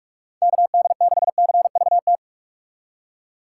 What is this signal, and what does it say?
Morse code, heard as a keyed tone, about 41 wpm, 700 Hz. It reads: KD6CVT
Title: USA Call Signs - 6 Characters - 3x Character Spacing - 45wpm